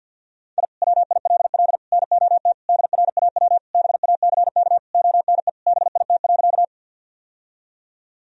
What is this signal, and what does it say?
Morse code, heard as a keyed tone, about 47 wpm, 700 Hz. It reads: I WILL NOT DRAW BACK ONE BIT.
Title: Sentences from Top 1000 Words - Review Part 1 - 50wpm